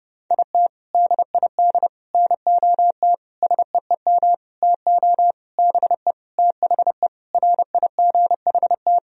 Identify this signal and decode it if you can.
Morse code, heard as a keyed tone, about 32 wpm, 700 Hz. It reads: IT DID NOT SEEM TO BE THE RIGHT
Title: Sentences from Top 1000 Words - Review Part 2 - 30wpm